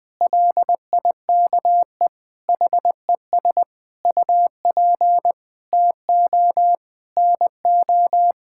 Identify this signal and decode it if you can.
Morse code, heard as a keyed tone, about 21 wpm, 700 Hz. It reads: LIKE HES UP TO NO
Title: Sets of 9 Words - 20wpm